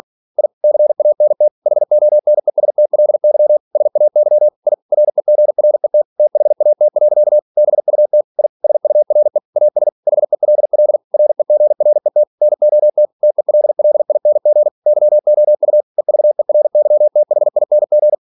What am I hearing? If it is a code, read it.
Morse code, heard as a keyed tone, about 46 wpm, 600 Hz. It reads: I CANT HONESTLY SAY I REGRET THAT. BUT I SURE AS HELL REGRET NOT TELLING YOU EVERYTHING
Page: Never 47wpm